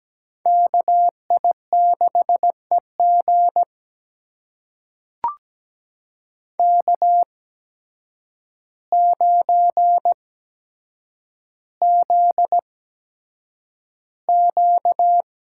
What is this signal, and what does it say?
Morse code, heard as a keyed tone, about 17 wpm, spaced as for 8 wpm, 700 Hz. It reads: KI6EG K9ZQ
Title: USA Call Signs - 8x Character Spacing - 17wpm